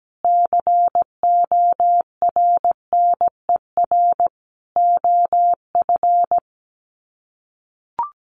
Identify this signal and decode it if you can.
Morse code, heard as a keyed tone, about 17 wpm, 700 Hz. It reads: CORNER OF
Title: Sets of 5 Words 17wpm